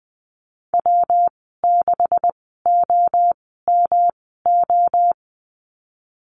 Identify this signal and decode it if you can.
Morse code, heard as a keyed tone, about 20 wpm, spaced as for 15 wpm, 700 Hz. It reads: W6OMO
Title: 140 Lesson 23g - Letter f - Call Signs 20wpm